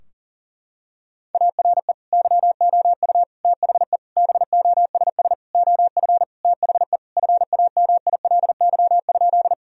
Morse code, heard as a keyed tone, about 39 wpm, 700 Hz. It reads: ARE YOU THE BOSS OF THE FAMILY?
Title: Sentences from Top 1000 Words - Review Part 4 - 40wpm